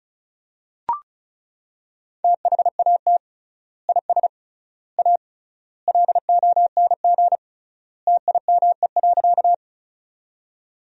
Morse code, heard as a keyed tone, about 36 wpm, 700 Hz. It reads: THAT IS A LONG TIME.
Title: 3x Word-Spacing - Sentences from Top 100 Words - 35wpm